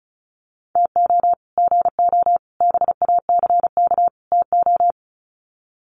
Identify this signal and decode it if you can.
Morse code, heard as a keyed tone, about 34 wpm, 700 Hz. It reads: TO GO BACK TO
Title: Sets of 4 Words 35wpm